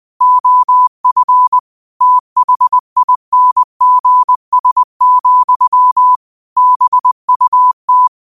Morse code, heard as a keyed tone, about 20 wpm, 1000 Hz. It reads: OF THINGS, BUT